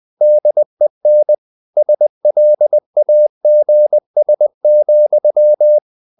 Morse code, heard as a keyed tone, about 20 wpm, 600 Hz. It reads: DEN SLAGS,
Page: Ferie 20 wpm